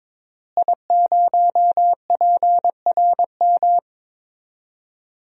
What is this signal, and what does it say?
Morse code, heard as a keyed tone, about 21 wpm, 700 Hz. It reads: I0PRM
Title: DXCC-100 Call Signs - 5 Characters - Repeated 3x - 22wpm